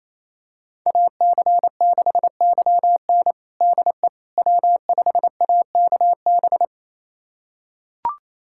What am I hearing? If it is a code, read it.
Morse code, heard as a keyed tone, about 28 wpm, 700 Hz. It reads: AC6YN DE W5AKB
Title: USA Callsign DE Callsign - 5 Characters - 28wpm